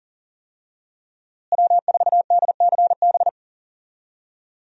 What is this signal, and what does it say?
Morse code, heard as a keyed tone, about 38 wpm, 700 Hz. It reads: W4DCB
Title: USA Call Signs - 5 Characters - Repeated 3x - 40wpm